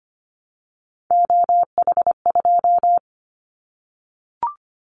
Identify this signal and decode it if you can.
Morse code, heard as a keyed tone, about 25 wpm, 700 Hz. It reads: OH2
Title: Callsign Prefixes for Top 50 DXCC - 25wpm